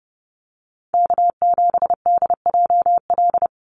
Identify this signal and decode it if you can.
Morse code, heard as a keyed tone, about 30 wpm, 700 Hz. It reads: K7DJL
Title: NA SOTA Chaser Callsigns 30wpm